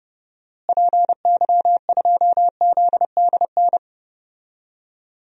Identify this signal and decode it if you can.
Morse code, heard as a keyed tone, about 32 wpm, 700 Hz. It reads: PY2ZDN
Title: DXCC-100 Call Signs - 6 Characters - Repeated 3x - 30wpm